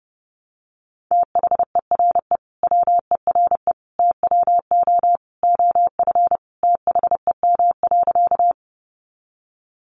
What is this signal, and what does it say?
Morse code, heard as a keyed tone, about 30 wpm, 700 Hz. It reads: THERE WERE TWO OF THEM.
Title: Sentences from Top 100 Words 30wpm